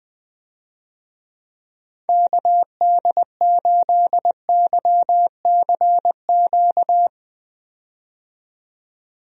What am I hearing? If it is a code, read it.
Morse code, heard as a keyed tone, about 21 wpm, 700 Hz. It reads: KD8YCQ
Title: USA Call Signs - 6 Characters - 20wpm